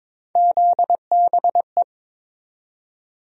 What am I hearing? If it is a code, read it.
Morse code, heard as a keyed tone, about 22 wpm, 700 Hz. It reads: ZBE
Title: USA Call Sign Postfixes - 3 Letters 22wpm